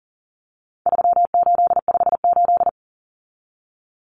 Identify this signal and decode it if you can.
Morse code, heard as a keyed tone, about 40 wpm, 700 Hz. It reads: 3858
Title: Common Cut Numbers - 4-Digit 40wpm